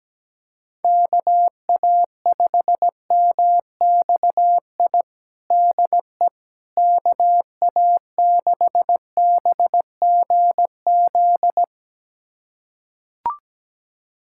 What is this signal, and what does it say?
Morse code, heard as a keyed tone, about 17 wpm, 700 Hz. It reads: KA5MXI DE KA6BGZ
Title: USA Callsign DE Callsign - 6 Characters - 17wpm